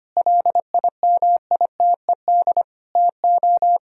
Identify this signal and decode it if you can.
Morse code, heard as a keyed tone, about 25 wpm, 700 Hz. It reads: LIMITED TO
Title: Sentences from Top 1000 Words - Review Part 6 - 25wpm